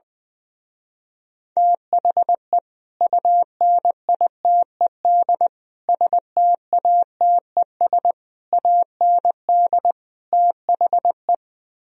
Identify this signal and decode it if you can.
Morse code, heard as a keyed tone, about 21 wpm, 700 Hz. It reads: THE UNITED STATES AND THE
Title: Speed-Racing - Sets of 5 Words 20wpm